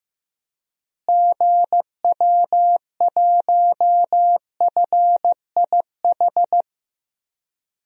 Morse code, heard as a keyed tone, about 15 wpm, 700 Hz. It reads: GW1FIH
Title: DXCC-100 Call Signs - 6 Characters - Repeated 3x - 15wpm